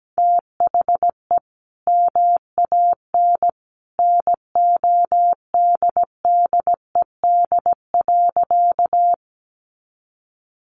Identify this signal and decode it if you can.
Morse code, heard as a keyed tone, about 17 wpm, 700 Hz. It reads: THE MAN NODDED.
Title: A Sound Of Thunder 17wpm